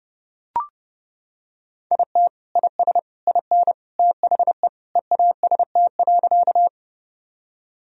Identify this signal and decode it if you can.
Morse code, heard as a keyed tone, about 30 wpm, 700 Hz. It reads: IT IS IN THE EAST.
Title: Sentences from Top 1000 Words - Review Part 3 - 30wpm